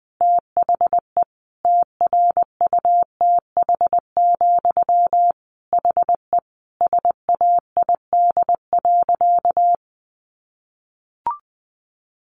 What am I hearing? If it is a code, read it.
Morse code, heard as a keyed tone, about 20 wpm, 700 Hz. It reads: THE TRUTH, HE SAID.
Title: Sentences from Top 800 Words 20wpm